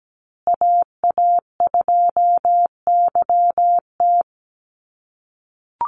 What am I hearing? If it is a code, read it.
Morse code, heard as a keyed tone, about 17 wpm, 700 Hz. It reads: AA2YT
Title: USA Call Signs Repeated 3x 17wpm